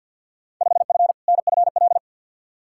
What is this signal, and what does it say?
Morse code, heard as a keyed tone, about 52 wpm, 700 Hz. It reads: 5F NFL
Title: Field Day Exchanges Encore - 50wpm